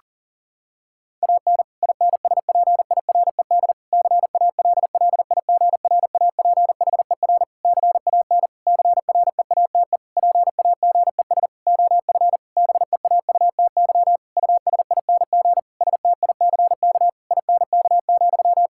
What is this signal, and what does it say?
Morse code, heard as a keyed tone, about 40 wpm, 700 Hz. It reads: AN INSPIRED CALLIGRAPHER CAN CREATE PAGES OF BEAUTY USING STICK INK,
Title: Pangram Challenge v2 - 40wpm